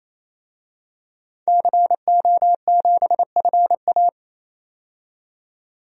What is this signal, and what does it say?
Morse code, heard as a keyed tone, about 28 wpm, 700 Hz. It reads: CO7FA
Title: DXCC-100 Call Signs - 5 Characters - 28wpm